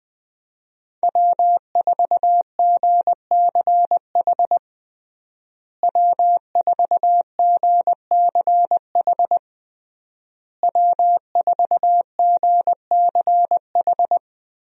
Morse code, heard as a keyed tone, about 20 wpm, 700 Hz. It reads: W4GCH W4GCH W4GCH
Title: USA Call Signs - 5 Characters - Repeated 3x - 20wpm